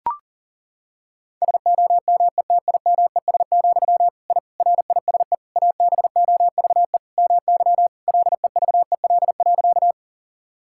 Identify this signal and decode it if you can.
Morse code, heard as a keyed tone, about 40 wpm, 700 Hz. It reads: SOMETIMES, I RISE ABOVE MY LEVEL.
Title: Sentences from Top 1000 Words - Review Part 6 - 40wpm